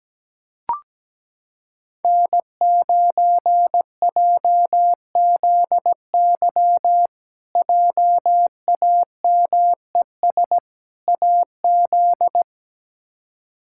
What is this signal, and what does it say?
Morse code, heard as a keyed tone, about 17 wpm, 700 Hz. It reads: N9JZY JAMES AZ
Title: K1USN SST QSO Practice - 17wpm